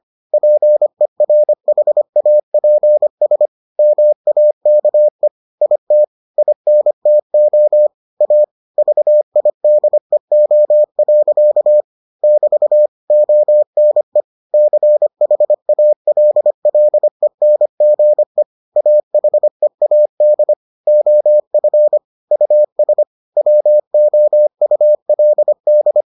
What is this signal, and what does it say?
Morse code, heard as a keyed tone, about 25 wpm, 600 Hz. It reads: PERHAPS MAKE IT INTO A VIDEO. <BT> ONE CHALLENGE AHEAD OF US WOULD